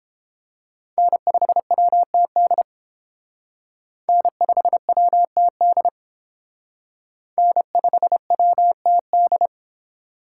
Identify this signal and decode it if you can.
Morse code, heard as a keyed tone, about 30 wpm, 700 Hz. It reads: N5WTD N5WTD N5WTD